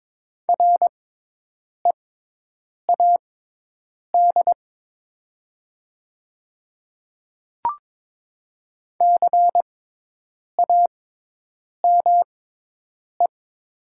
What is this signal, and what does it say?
Morse code, heard as a keyed tone, about 22 wpm, spaced as for 8 wpm, 700 Hz. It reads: READ CAME
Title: 6x Character Spacing - Top 200 Words - 22wpm